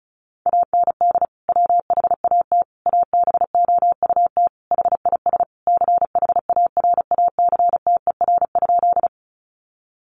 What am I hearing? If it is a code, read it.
Morse code, heard as a keyed tone, about 34 wpm, 700 Hz. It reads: AND WHAT ABOUT HIS CHARACTER?
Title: Sentences from Top 1200 Words 35wpm